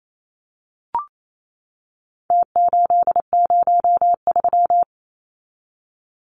Morse code, heard as a keyed tone, about 28 wpm, 700 Hz. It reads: T803
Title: Numbers - Contest Style - 4-Digit 28wpm